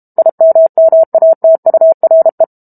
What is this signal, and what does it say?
Morse code, heard as a keyed tone, about 32 wpm, 650 Hz. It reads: IMMATURE
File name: Chapter 6 - 32 WPM